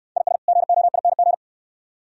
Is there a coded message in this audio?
Yes: Morse code from 0.2 to 1.3 s, heard as a keyed tone, about 69 wpm, 690 Hz. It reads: IS GOING